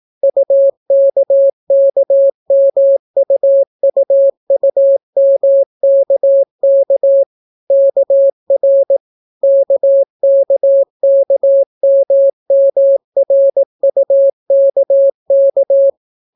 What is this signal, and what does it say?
Morse code, heard as a keyed tone, about 18 wpm, 550 Hz. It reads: UKKMUUUMKK KR KKKMMRUKK